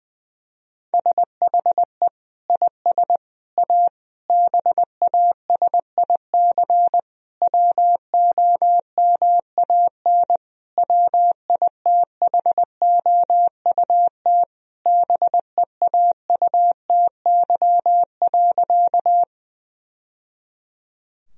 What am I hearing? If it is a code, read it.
Morse code, heard as a keyed tone, about 20 wpm, 700 Hz. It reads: SHE IS A BASIC WOMAN WITHOUT BEAUTY.
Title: Sentences from Top 1000 Words - Review Part 3 - 20wpm